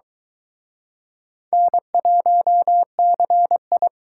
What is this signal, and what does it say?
Morse code, heard as a keyed tone, about 23 wpm, 700 Hz. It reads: N1CI